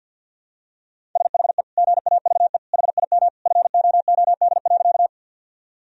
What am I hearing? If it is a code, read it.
Morse code, heard as a keyed tone, about 51 wpm, 700 Hz. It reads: SHE GAVE HIM FOOD.
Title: Sentences from Top 1000 Words - Review Part 1 - 50wpm